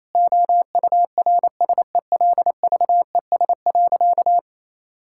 Morse code, heard as a keyed tone, about 28 wpm, 700 Hz. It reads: OURSELVES.
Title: Sentences from Top 1600 Words 28wpm